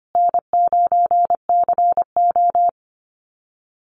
Morse code, heard as a keyed tone, about 25 wpm, 700 Hz. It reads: N9CO
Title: Call Signs Repeated 2x - 25wpm